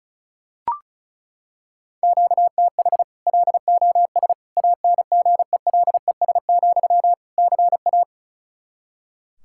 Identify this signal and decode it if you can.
Morse code, heard as a keyed tone, about 36 wpm, 700 Hz. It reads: QTH LOS ANGELES, CA
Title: Practice QSO - Exchange 1 - Beginning - 35wpm